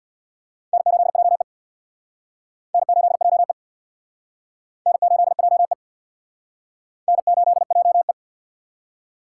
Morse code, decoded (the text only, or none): N8JE N8JE N8JE N8JE